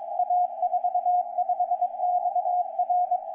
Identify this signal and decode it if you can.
Morse code, heard as a keyed tone, about 23 wpm, 720 Hz. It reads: A45XR
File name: A45XR-160-CW-2DIC16